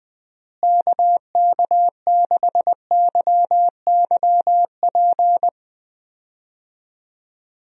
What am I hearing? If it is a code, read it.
Morse code, heard as a keyed tone, about 20 wpm, 700 Hz. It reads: KK6YYP
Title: USA Call Signs - 6 Characters - Repeated 2x - 20wpm